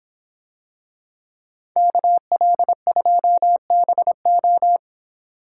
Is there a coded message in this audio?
Yes: Morse code from 1.8 to 4.8 s, heard as a keyed tone, about 26 wpm, 700 Hz. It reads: KL2BO